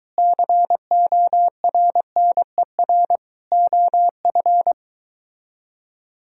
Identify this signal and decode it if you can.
Morse code, heard as a keyed tone, about 22 wpm, 700 Hz. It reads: CORNER OF